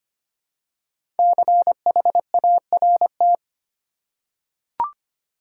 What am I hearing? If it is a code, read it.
Morse code, heard as a keyed tone, about 25 wpm, 700 Hz. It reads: CHART
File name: 5 Letter Words - First 3 Letters Given - 25wpm